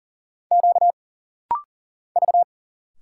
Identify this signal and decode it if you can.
Morse code, heard as a keyed tone, about 40 wpm, 700 Hz. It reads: Q V
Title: Single Letters - Rapid-Fire 40wpm